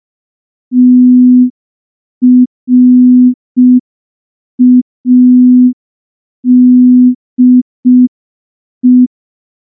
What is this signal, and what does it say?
Morse code, heard as a keyed tone, about 5 wpm, 250 Hz. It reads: TRADE
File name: morse / morse-training-05WPM